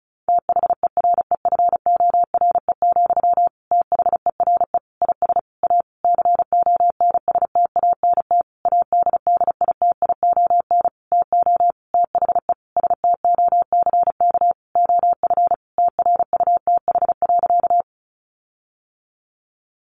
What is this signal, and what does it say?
Morse code, heard as a keyed tone, about 34 wpm, 700 Hz. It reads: THEREFORE, THERE IS A CONSTANT ADDITION TO THE STOCK OF TRUTH.
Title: Sentences from Top 1500 Words 35wpm